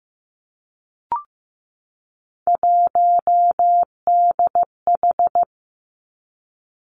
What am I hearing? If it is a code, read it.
Morse code, heard as a keyed tone, about 15 wpm, 700 Hz. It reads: 1DH
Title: Three Challenging Letter-Number Combinations 15wpm